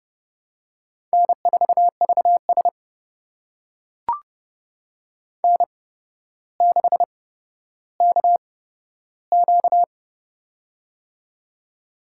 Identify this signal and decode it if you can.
Morse code, heard as a keyed tone, about 32 wpm, spaced as for 13 wpm, 700 Hz. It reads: N4VS N6KQ
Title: USA Call Signs - 4 Characters - 8x Character Spacing - 30wpm